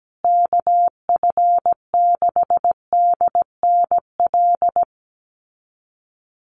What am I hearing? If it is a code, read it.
Morse code, heard as a keyed tone, about 17 wpm, 700 Hz. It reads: KF6DNL
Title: USA Call Signs Repeated 3x 17wpm